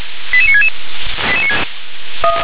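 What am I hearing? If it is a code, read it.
Five-tone selective call (ZVEI), heard as quick runs of five tones, tones of about 80 ms. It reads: 88966
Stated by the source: Five-tone ZVEI selective call 88960 received by SDR.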